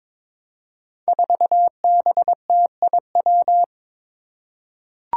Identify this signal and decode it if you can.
Morse code, heard as a keyed tone, about 21 wpm, 700 Hz. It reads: 4BTIW
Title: Five Challenging Letter-Number Combinations 22wpm